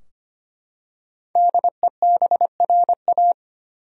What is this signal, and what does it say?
Morse code, heard as a keyed tone, about 25 wpm, 700 Hz. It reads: DEBRA
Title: USA Common Names - 25wpm